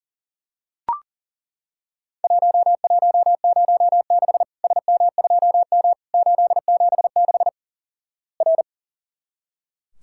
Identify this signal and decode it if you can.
Morse code, heard as a keyed tone, about 37 wpm, 700 Hz. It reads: 1106 SM2M 876 R
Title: Worked All Europe DX Contest - QTC Exchange - 40wpm